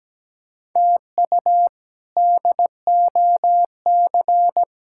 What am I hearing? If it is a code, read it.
Morse code, heard as a keyed tone, about 17 wpm, 700 Hz. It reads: TU DOC
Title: K1USN SST QSO Practice - 17wpm